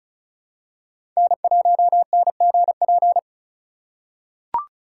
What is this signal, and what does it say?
Morse code, heard as a keyed tone, about 34 wpm, 700 Hz. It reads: N1NGP
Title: USA Call Signs - 5 Characters - 3x Character Spacing - 35wpm